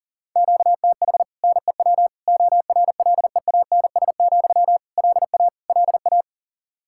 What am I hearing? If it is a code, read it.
Morse code, heard as a keyed tone, about 40 wpm, 700 Hz. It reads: QTH NEW ORLEANS, LA LA
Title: Practice QSO - Exchange 1 - Beginning - 40wpm